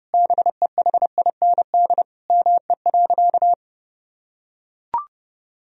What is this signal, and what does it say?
Morse code, heard as a keyed tone, about 30 wpm, 700 Hz. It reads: BEHIND ME.
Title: Sentences from Top 1000 Words - Review Part 1 - 30wpm